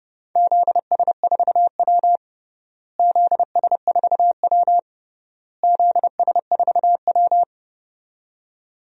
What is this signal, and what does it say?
Morse code, heard as a keyed tone, about 30 wpm, 700 Hz. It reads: ZS4W ZS4W ZS4W
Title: DXCC-100 Call Signs - 4 Characters - Repeated 3x - 30wpm